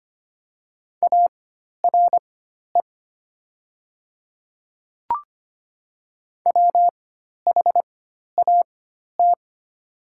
Morse code, heard as a keyed tone, about 25 wpm, spaced as for 12 wpm, 700 Hz. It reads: ARE WHAT
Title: 4x Character Spacing - Top 100 Words - 25wpm